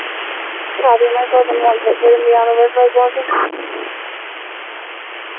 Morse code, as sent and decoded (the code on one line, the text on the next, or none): none